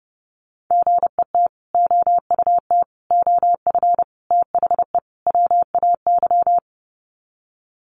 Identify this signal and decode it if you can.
Morse code, heard as a keyed tone, about 30 wpm, 700 Hz. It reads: GET OUT OF THE WAY
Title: Sets of 5 Words Encore 30wpm